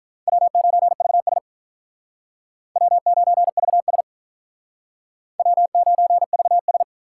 Morse code, heard as a keyed tone, about 44 wpm, 700 Hz. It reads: W9VS W9VS W9VS